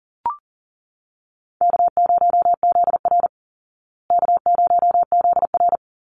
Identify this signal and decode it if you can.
Morse code, heard as a keyed tone, about 40 wpm, 700 Hz. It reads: K0ZR K0ZR
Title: Call Signs Repeated 3x - 40wpm